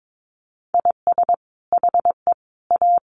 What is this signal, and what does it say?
Morse code, heard as a keyed tone, about 22 wpm, 700 Hz. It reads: IS HE A
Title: Sentences from Top 1500 Words 22wpm